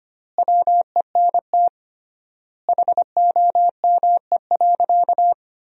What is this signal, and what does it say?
Morse code, heard as a keyed tone, about 25 wpm, 700 Hz. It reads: WENT HOME.
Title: 3x Word-Spacing - Sentences from Top 200 Words - 25wpm